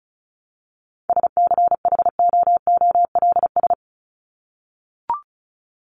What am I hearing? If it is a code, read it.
Morse code, heard as a keyed tone, about 34 wpm, 700 Hz. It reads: SCHOOLS
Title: ES:S Suffix - Top 100 Words with Base Words 35wpm